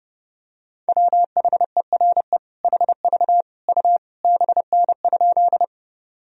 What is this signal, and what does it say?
Morse code, heard as a keyed tone, about 32 wpm, 700 Hz. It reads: WHERE HV U BN?
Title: Sentences from Top 200 Words and Abbreviations 30wpm